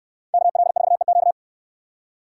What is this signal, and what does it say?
Morse code, heard as a keyed tone, about 69 wpm, 690 Hz. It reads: KL4EQ